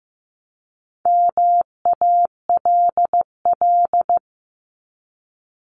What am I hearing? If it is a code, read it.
Morse code, heard as a keyed tone, about 15 wpm, 700 Hz. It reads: MALL
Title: Top 1501-1750 Words 15wpm